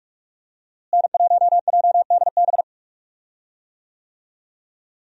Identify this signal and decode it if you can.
Morse code, heard as a keyed tone, about 42 wpm, 700 Hz. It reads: N1JDB
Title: USA Call Signs - 5 Characters - Repeated 2x - 45wpm